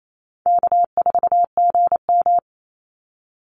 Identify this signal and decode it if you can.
Morse code, heard as a keyed tone, about 28 wpm, 700 Hz. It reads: K4GM
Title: Easy Call Signs Repeated 2x - 28wpm